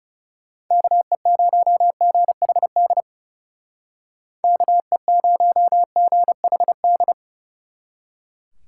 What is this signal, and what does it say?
Morse code, heard as a keyed tone, about 36 wpm, 700 Hz. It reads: KE0GHD KE0GHD
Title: Speed Racing - USA 2x3 Call Signs 30wpm